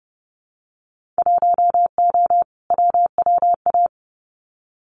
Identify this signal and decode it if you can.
Morse code, heard as a keyed tone, about 30 wpm, 700 Hz. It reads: 1O WWA
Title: Winter Field Day Exchanges 30wpm